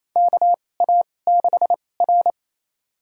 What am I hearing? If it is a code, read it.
Morse code, heard as a keyed tone, about 28 wpm, spaced as for 20 wpm, 700 Hz. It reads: KA6R
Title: USA Call Signs - 2x Character Spacing - 28wpm